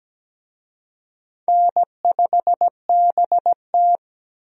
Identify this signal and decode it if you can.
Morse code, heard as a keyed tone, about 17 wpm, 700 Hz. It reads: N5BT
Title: USA Call Signs - 4 Characters - Repeated 2x - 17wpm